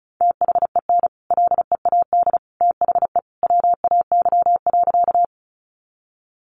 Morse code, heard as a keyed tone, about 34 wpm, 700 Hz. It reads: THEN LEAD THE WAY.
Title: Sentences from Top 500 Words 35wpm